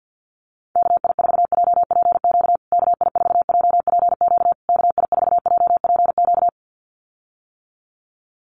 Morse code, heard as a keyed tone, about 50 wpm, 700 Hz. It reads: KI4JPQ KI4JPQ KI4JPQ
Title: USA Call Signs Repeated 3x 50wpm